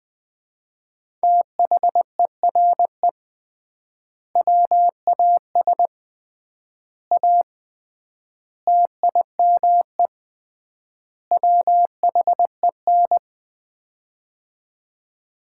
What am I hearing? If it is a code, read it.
Morse code, heard as a keyed tone, about 20 wpm, 700 Hz. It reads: THERE WAS A TIME WHEN
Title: 3x Word-Spacing - Sets of 5 Words - 20wpm